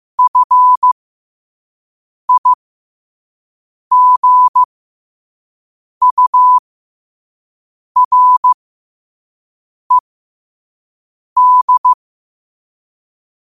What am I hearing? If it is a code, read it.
Morse code, heard as a keyed tone, about 15 wpm, spaced as for 5 wpm, 1000 Hz. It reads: FIGURED
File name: qotdfiles / 5wpm